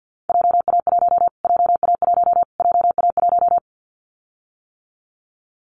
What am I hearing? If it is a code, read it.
Morse code, heard as a keyed tone, about 50 wpm, 700 Hz. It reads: JA1 JA1 JA1
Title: Callsign Prefixes for Top 25 DXCC - Repeated 3x - 50wpm